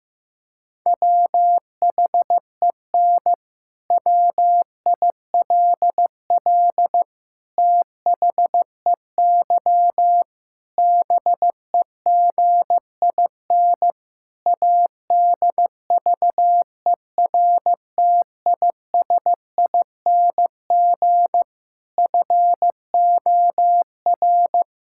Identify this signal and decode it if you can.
Morse code, heard as a keyed tone, about 15 wpm, 700 Hz. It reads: WHEN WILL THEY BEGIN ADVERTISING FOR
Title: Sentences from Top 1000 Words - Review Part 6 - 15wpm